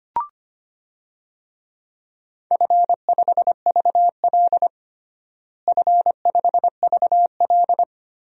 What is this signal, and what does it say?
Morse code, heard as a keyed tone, about 25 wpm, 700 Hz. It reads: F5VL F5VL
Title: DXCC-100 Call Signs - 4 Characters - Repeated 3x - 25wpm